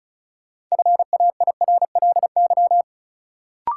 Morse code, heard as a keyed tone, about 34 wpm, 700 Hz. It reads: FAIRLY
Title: Sentences from Top 1100 Words - New Word Drill - 35wpm